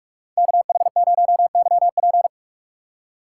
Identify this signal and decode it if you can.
Morse code, heard as a keyed tone, about 46 wpm, 700 Hz. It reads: KH0YP
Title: USA Call Signs - 5 Characters - 2x Character Spacing - 45wpm